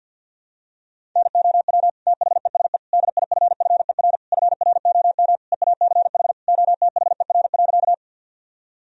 Morse code, heard as a keyed tone, about 52 wpm, 700 Hz. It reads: NOW THESE DIFFER FROM EACH OTHER.
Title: Sentences from Top 1000 Words - Review Part 1 - 50wpm